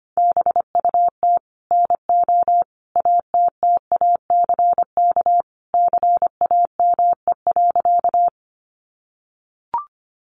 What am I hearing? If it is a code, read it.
Morse code, heard as a keyed tone, about 25 wpm, 700 Hz. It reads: BUT NO ATTACK CAME.
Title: Sentences from Top 1500 Words 25wpm